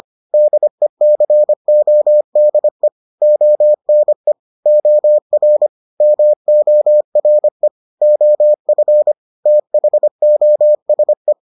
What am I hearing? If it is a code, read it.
Morse code, heard as a keyed tone, about 25 wpm, 600 Hz. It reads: DECODE ONE OR MORE OF THOSE